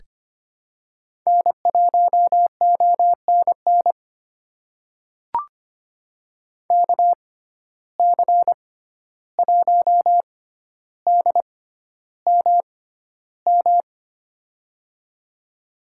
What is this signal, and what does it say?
Morse code, heard as a keyed tone, about 25 wpm, spaced as for 12 wpm, 700 Hz. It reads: N1ONN KC1DMM
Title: USA Call Signs - 6x Character Spacing - 25wpm